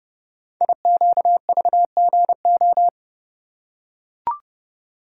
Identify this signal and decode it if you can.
Morse code, heard as a keyed tone, about 30 wpm, 700 Hz. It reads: IQVGO
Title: Code Groups - 5 Letters - 30wpm